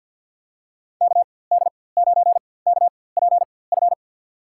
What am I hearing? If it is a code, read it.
Morse code, heard as a keyed tone, about 48 wpm, spaced as for 23 wpm, 700 Hz. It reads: KD9KPF
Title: USA Call Signs - 6 Characters - 4x Character Spacing - 50wpm